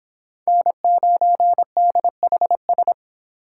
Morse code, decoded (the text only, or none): N9DHS